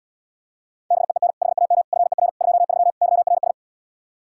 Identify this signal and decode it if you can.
Morse code, heard as a keyed tone, about 77 wpm, 700 Hz. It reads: BEEN HAM FER 23 YRS